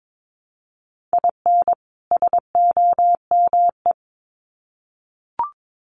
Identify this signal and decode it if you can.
Morse code, heard as a keyed tone, about 22 wpm, 700 Hz. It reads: IN SOME
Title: Sets of 2 Words 22wpm